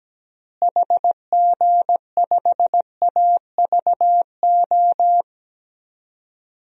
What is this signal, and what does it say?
Morse code, heard as a keyed tone, about 17 wpm, 700 Hz. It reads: HG5AVO
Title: DXCC-100 Call Signs - 6 Characters - Repeated 2x - 17wpm